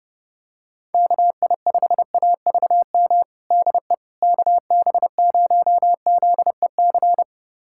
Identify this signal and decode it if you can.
Morse code, heard as a keyed tone, about 30 wpm, 700 Hz. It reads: KI5AVM DE KB0ZEC
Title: USA Callsign DE Callsign - 6 Characters - 30wpm